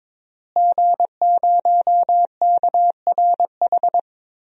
Morse code, heard as a keyed tone, about 22 wpm, 700 Hz. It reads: G0KRH
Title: DXCC-100 Call Signs - 5 Characters - Repeated 2x - 22wpm